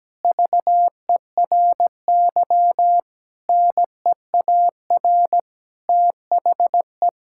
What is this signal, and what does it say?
Morse code, heard as a keyed tone, about 17 wpm, 700 Hz. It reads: VERY NEAR THE
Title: Sentences from Top 1000 Words - Review Part 6 - 17wpm